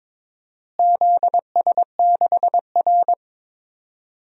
Morse code, heard as a keyed tone, about 22 wpm, 700 Hz. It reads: ZS6R
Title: DXCC-100 Call Signs - 4 Characters - Repeated 2x - 22wpm